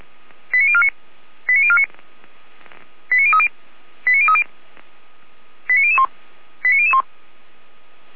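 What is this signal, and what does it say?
Five-tone selective call (ZVEI), heard as quick runs of five tones, tones of about 70 ms. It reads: 89049 89049 89030 89030 89001 89001
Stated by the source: ZVEI five-tone selective call received on an SDR.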